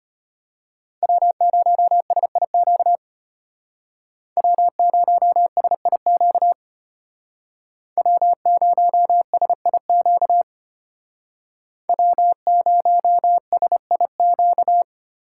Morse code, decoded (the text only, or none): W0SIQ W0SIQ W0SIQ W0SIQ